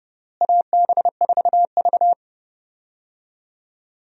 Morse code, heard as a keyed tone, about 30 wpm, 700 Hz. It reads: AB4V
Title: USA Call Signs - 4 Characters - Repeated 2x - 30wpm